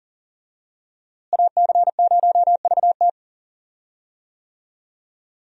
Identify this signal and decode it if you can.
Morse code, heard as a keyed tone, about 40 wpm, 700 Hz. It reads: AC0VT
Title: USA Call Signs - 5 Characters - Repeated 2x - 40wpm